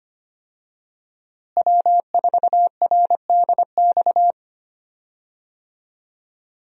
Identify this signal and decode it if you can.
Morse code, heard as a keyed tone, about 25 wpm, 700 Hz. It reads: W4RDX
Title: USA Call Signs - 5 Characters - Repeated 2x - 25wpm